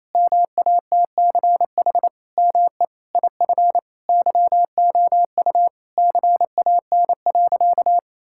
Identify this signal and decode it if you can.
Morse code, heard as a keyed tone, about 28 wpm, 700 Hz. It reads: MATCH ME IF YOU CAN.
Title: Sentences from Top 1600 Words 28wpm